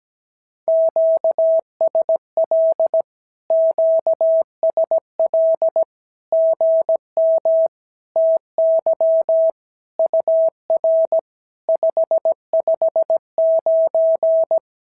Morse code, heard as a keyed tone, about 17 wpm, 650 Hz. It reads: QSL QSL GM TY UR 559
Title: SOTA QSO Practice 17wpm